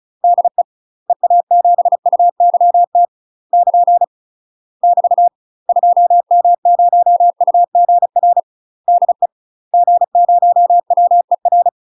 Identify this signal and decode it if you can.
Morse code, heard as a keyed tone, about 34 wpm, 700 Hz. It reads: DE EA7UYT <KN> <BT> 2M0UGR DE G0WER
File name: UK MASTER QSOs TRAINING / 35WPM / UK MASTER QSOs 0042